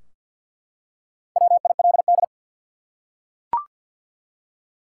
Morse code, heard as a keyed tone, about 48 wpm, 700 Hz. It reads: WILD
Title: Four Letter Words - 50wpm